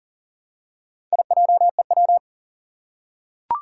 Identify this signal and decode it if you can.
Morse code, heard as a keyed tone, about 39 wpm, 700 Hz. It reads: IJEW
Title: Four Challenging Letter-Number Combinations 40wpm